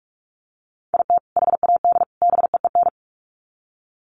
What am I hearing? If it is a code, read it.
Morse code, heard as a keyed tone, about 44 wpm, 700 Hz. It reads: IT HAD BEEN
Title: Sets of 3 Words 45wpm